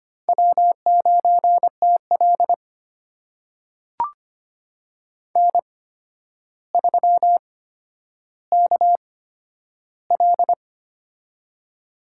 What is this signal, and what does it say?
Morse code, heard as a keyed tone, about 25 wpm, spaced as for 11 wpm, 700 Hz. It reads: W9TL N3KL
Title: USA Call Signs - 4 Characters - 8x Character Spacing - 25wpm